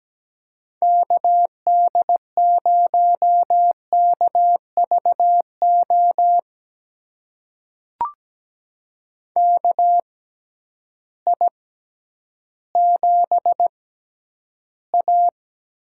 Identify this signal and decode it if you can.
Morse code, heard as a keyed tone, about 17 wpm, spaced as for 10 wpm, 700 Hz. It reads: KD0KVO KI7A
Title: USA Call Signs - 6 Characters - 6x Character Spacing - 17wpm